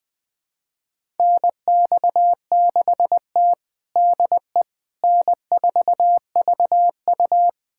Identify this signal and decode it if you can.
Morse code, heard as a keyed tone, about 20 wpm, 700 Hz. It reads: NX6T DE N4VU
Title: USA Callsign DE Callsign - 4 Characters - 20wpm